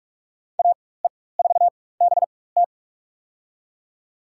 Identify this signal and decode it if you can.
Morse code, heard as a keyed tone, about 46 wpm, spaced as for 21 wpm, 700 Hz. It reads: AE4BT
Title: USA Call Signs - 5 Characters - 4x Character Spacing - 45wpm